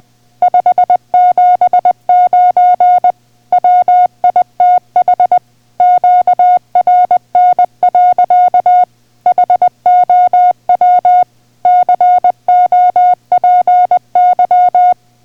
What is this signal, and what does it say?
Morse code, heard as a keyed tone, about 20 wpm, 720 Hz. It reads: 579 WITH QRN. HOW COPY